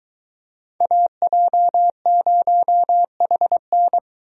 Morse code, heard as a keyed tone, about 22 wpm, 700 Hz. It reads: AJ0HN